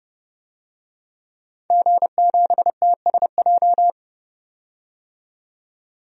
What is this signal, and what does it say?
Morse code, heard as a keyed tone, about 30 wpm, 700 Hz. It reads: G7TSJ
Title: DXCC-100 Call Signs - 5 Characters - 30wpm